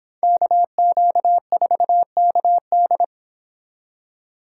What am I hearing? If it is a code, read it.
Morse code, heard as a keyed tone, about 29 wpm, 700 Hz. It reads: KQ4KD